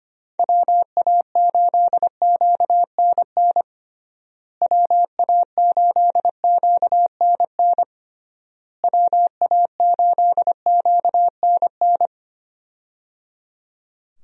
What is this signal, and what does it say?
Morse code, heard as a keyed tone, about 25 wpm, 700 Hz. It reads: WA8QNN WA8QNN WA8QNN
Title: USA Call Signs - 6 Characters - Repeated 3x - 25wpm